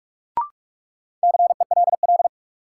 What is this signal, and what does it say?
Morse code, heard as a keyed tone, about 47 wpm, 700 Hz. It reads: CELL
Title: Sentences from Top 1100 Words - New Word Drill - 45wpm